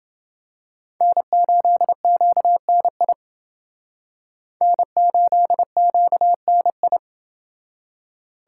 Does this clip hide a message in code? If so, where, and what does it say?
Yes: Morse code from 1.0 to 7.0 s, heard as a keyed tone, about 28 wpm, 700 Hz. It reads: N8QNI N8QNI